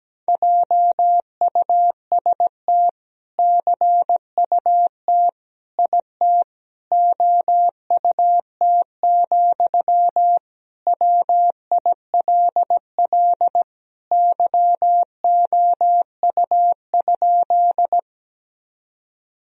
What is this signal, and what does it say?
Morse code, heard as a keyed tone, about 17 wpm, 700 Hz. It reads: JUST CUT IT OUT, WILL YOU?
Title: Sentences from Top 1000 Words - Review Part 1 - 17wpm